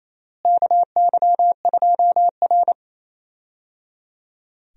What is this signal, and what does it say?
Morse code, heard as a keyed tone, about 28 wpm, 700 Hz. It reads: KY2R
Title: USA Call Signs - 4 Characters - Repeated 3x - 28wpm